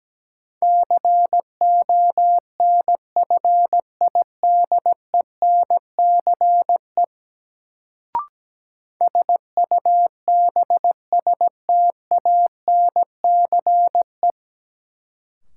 Morse code, heard as a keyed tone, about 17 wpm, 700 Hz. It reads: CONFIDENCE SUBSTANCE
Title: Sentences from Top 1500 Words - New Word Drill - 17wpm